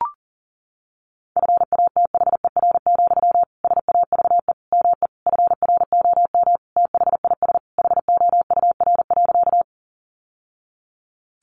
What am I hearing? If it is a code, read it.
Morse code, heard as a keyed tone, about 40 wpm, 700 Hz. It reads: FATHER, SAVE ME FROM THIS HOUR.
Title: Sentences from Top 1100 Words 40wpm